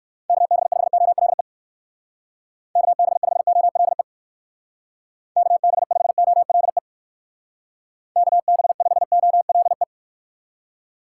Morse code, heard as a keyed tone, about 56 wpm, 700 Hz. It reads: KB5OLE KB5OLE KB5OLE KB5OLE
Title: Speed Racing - USA 2x3 Call Signs 45wpm